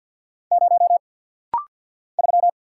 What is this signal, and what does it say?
Morse code, heard as a keyed tone, about 48 wpm, 700 Hz. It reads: 0 3
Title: Single Letter-Number - Rapid-Fire 50wpm